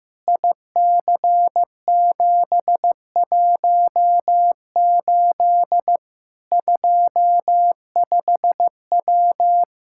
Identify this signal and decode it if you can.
Morse code, heard as a keyed tone, about 15 wpm, 700 Hz. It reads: IC718 25W
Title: Practice QSO - Exchange 2 - Info - 15wpm